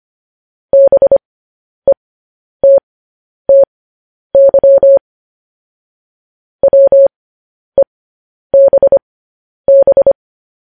Morse code, heard as a keyed tone, about 25 wpm, spaced as for 10 wpm, 550 Hz. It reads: BETTY WEBB
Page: News Headlines in Morse Code at 10 WPM